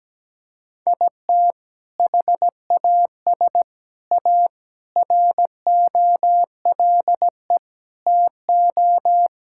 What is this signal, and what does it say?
Morse code, heard as a keyed tone, about 17 wpm, 700 Hz. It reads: IT HAS A ROLE TO
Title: Sentences from Top 1000 Words - Review Part 6 - 17wpm